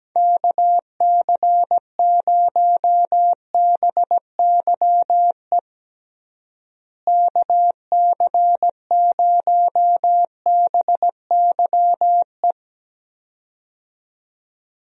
Morse code, heard as a keyed tone, about 17 wpm, 700 Hz. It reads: KC0BYE KC0BYE
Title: USA Call Signs - 6 Characters - Repeated 2x - 17wpm